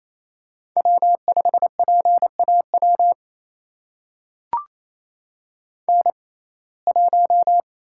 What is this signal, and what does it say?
Morse code, heard as a keyed tone, about 28 wpm, spaced as for 20 wpm, 700 Hz. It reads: W5PAW N1
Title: USA Call Signs - 5 Characters - 6x Character Spacing - 28wpm